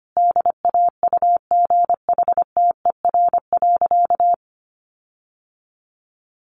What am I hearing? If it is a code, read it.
Morse code, heard as a keyed tone, about 25 wpm, 700 Hz. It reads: DAUGHTER.
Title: Sentences from Top 1500 Words 25wpm